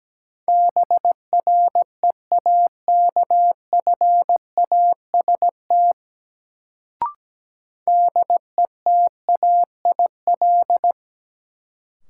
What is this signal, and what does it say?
Morse code, heard as a keyed tone, about 17 wpm, 700 Hz. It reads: BREAKFAST DETAIL
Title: Sentences from Top 1500 Words - New Word Drill Alt - 17wpm